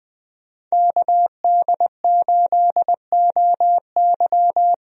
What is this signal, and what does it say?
Morse code, heard as a keyed tone, about 20 wpm, 700 Hz. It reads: KD8OY
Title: USA Call Signs - 5 Characters - Repeated 3x - 20wpm